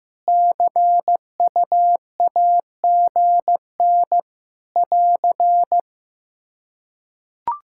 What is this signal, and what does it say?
Morse code, heard as a keyed tone, about 15 wpm, 700 Hz. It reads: CUAGN <AR>
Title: Practice QSO - Exchange 4 - Ending - 15wpm